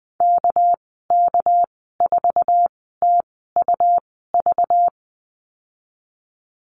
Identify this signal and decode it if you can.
Morse code, heard as a keyed tone, about 20 wpm, spaced as for 15 wpm, 700 Hz. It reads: KK4TUV
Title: 219 Lesson 35g - Number 0 - Call Signs 20wpm